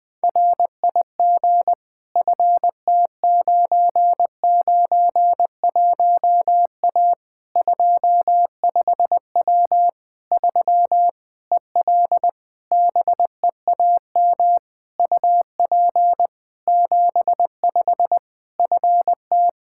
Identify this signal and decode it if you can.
Morse code, heard as a keyed tone, about 21 wpm, 700 Hz. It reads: RIG FT991A 25W 3 EL BEAM UP 75 FT
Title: K1USN SST QSO Practice 20wpm